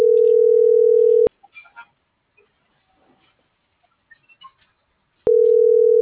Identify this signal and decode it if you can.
Ringback tone (North America), 440 + 480 Hz.